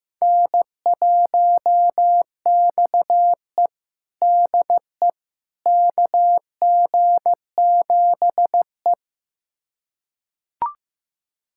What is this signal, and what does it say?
Morse code, heard as a keyed tone, about 15 wpm, 700 Hz. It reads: N1XE DE KG7E
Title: USA Callsign DE Callsign - 4 Characters - 15wpm